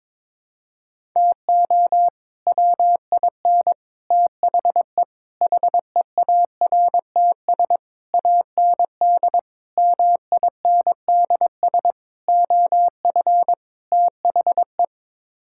Morse code, heard as a keyed tone, about 22 wpm, 700 Hz. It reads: TO WIN THE HEARTS AND MINDS OF THE
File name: practice / 22wpm / Sets of 8 Words - 22wpm